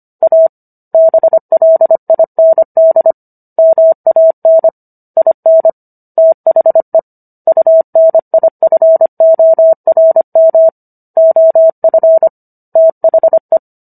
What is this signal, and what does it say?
Morse code, heard as a keyed tone, about 25 wpm, 650 Hz. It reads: A BLIND MAN IN THE UNIFORM OF THE